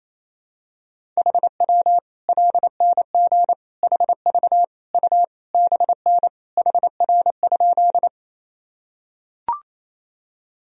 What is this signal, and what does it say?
Morse code, heard as a keyed tone, about 28 wpm, 700 Hz. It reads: HW LNG HV U BN HR?
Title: Sentences from Top 200 Words and Abbreviations 28wpm